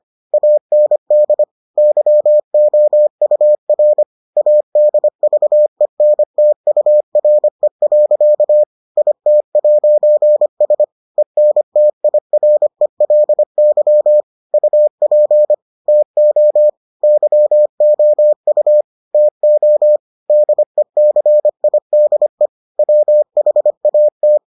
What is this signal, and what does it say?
Morse code, heard as a keyed tone, about 25 wpm, 600 Hz. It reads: AND YOUR ADVENTURE. IT'S ENTIRELY UP TO YOU TO DECIDE WHAT